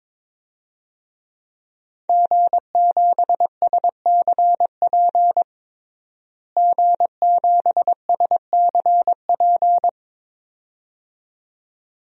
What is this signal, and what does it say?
Morse code, heard as a keyed tone, about 22 wpm, 700 Hz. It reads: G7SCP G7SCP
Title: DXCC-100 Call Signs - 5 Characters - Repeated 2x - 22wpm